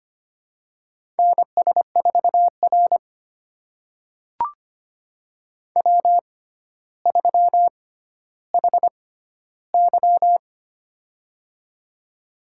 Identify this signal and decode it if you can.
Morse code, heard as a keyed tone, about 25 wpm, spaced as for 13 wpm, 700 Hz. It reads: NS4R W3HY
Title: USA Call Signs - 4 Characters - 6x Character Spacing - 25wpm